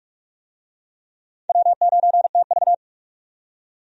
Morse code, heard as a keyed tone, about 46 wpm, 700 Hz. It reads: W9TV
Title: USA Call Signs - 4 Characters - 45wpm